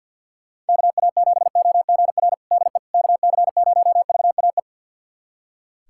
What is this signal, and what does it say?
Morse code, heard as a keyed tone, about 46 wpm, 700 Hz. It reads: KA7OGR DE KC0VAE
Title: USA Callsign DE Callsign - 6 Characters - 50wpm